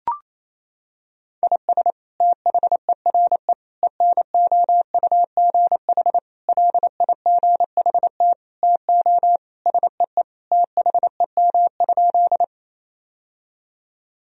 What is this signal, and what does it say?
Morse code, heard as a keyed tone, about 28 wpm, 700 Hz. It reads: IS THERE ENOUGH LIGHT TO SEE THEM?
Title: Sentences from Top 1000 Words - Review Part 1 - 28wpm